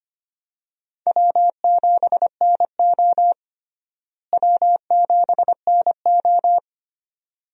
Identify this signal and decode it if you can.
Morse code, heard as a keyed tone, about 25 wpm, 700 Hz. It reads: W7NO W7NO
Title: USA Call Signs - 4 Characters - Repeated 2x - 25wpm